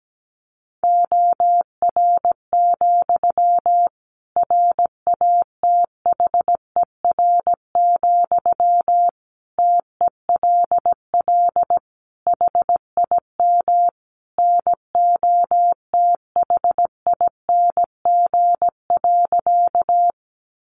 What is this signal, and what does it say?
Morse code, heard as a keyed tone, about 17 wpm, 700 Hz. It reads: OR, RATHER, TELL HIM NOTHING.
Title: Sentences from Top 700 Words 17wpm